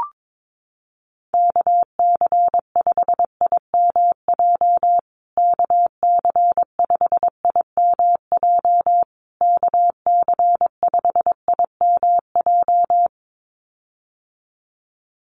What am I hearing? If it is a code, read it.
Morse code, heard as a keyed tone, about 22 wpm, 700 Hz. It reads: KC5IMJ KC5IMJ KC5IMJ
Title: USA Call Signs Repeated 3x 22wpm